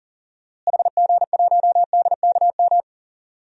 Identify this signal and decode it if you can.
Morse code, heard as a keyed tone, about 40 wpm, 700 Hz. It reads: HG1DKM
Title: DXCC-100 Call Signs - 6 Characters - Repeated 3x - 40wpm